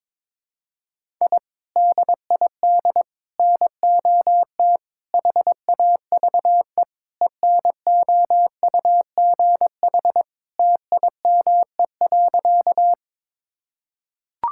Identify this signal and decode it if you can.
Morse code, heard as a keyed tone, about 21 wpm, 700 Hz. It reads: I DID NOT HAVE ENOUGH TIME.
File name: practice / 22wpm / Sentences with 4-Word Phrases - 22wpm